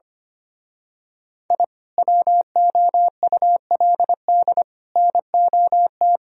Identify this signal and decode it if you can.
Morse code, heard as a keyed tone, about 24 wpm, 700 Hz. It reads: I WOULD NOT
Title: Speed-Racing - Sets of 5 Words 25wpm